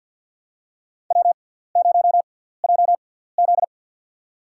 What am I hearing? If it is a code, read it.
Morse code, heard as a keyed tone, about 49 wpm, spaced as for 17 wpm, 700 Hz. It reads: W0JZ
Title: USA Call Signs - 4 Characters - 6x Character Spacing - 50wpm